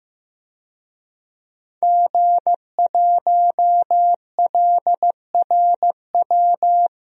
Morse code, heard as a keyed tone, about 15 wpm, 700 Hz. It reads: G1LRW
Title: DXCC-100 Call Signs - 5 Characters - Repeated 3x - 15wpm